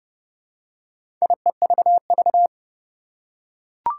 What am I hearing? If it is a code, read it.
Morse code, heard as a keyed tone, about 30 wpm, 700 Hz. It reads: IEVV
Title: Four Challenging Letter-Number Combinations 30wpm